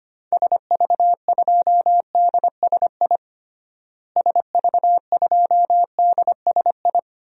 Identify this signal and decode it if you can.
Morse code, heard as a keyed tone, about 25 wpm, 700 Hz. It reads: SV2DSI SV2DSI
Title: DXCC-100 Call Signs - 6 Characters - Repeated 3x - 25wpm